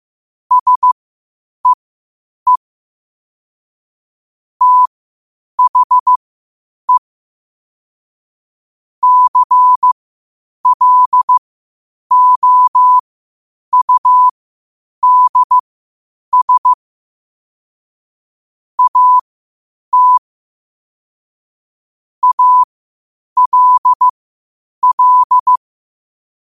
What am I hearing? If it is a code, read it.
Morse code, heard as a keyed tone, about 15 wpm, spaced as for 8 wpm, 1000 Hz. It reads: SEE THE CLOUDS AT ALL
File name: qotdfiles / 8wpm